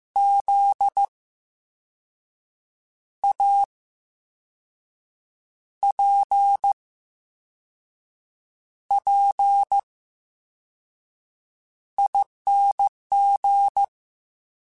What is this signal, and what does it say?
Morse code, heard as a keyed tone, about 15 wpm, spaced as for 5 wpm, 800 Hz. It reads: ZAPPING
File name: Special INGs / Special INGs-mp3s / Slow ing @13